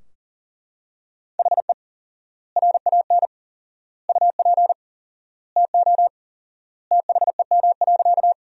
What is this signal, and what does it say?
Morse code, heard as a keyed tone, about 40 wpm, 700 Hz. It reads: HE RAN UP TO THEM.
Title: 4x Word-Spacing - Sentences from Top 500 Words - 40wpm